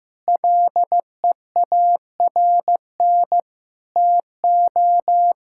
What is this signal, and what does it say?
Morse code, heard as a keyed tone, about 15 wpm, 700 Hz. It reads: LEARN TO
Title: Sentences from Top 1000 Words - Review Part 6 - 15wpm